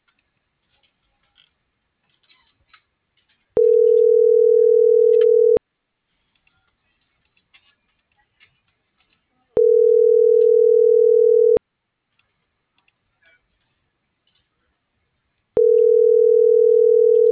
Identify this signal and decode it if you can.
Ringback tone (North America), 440 + 480 Hz.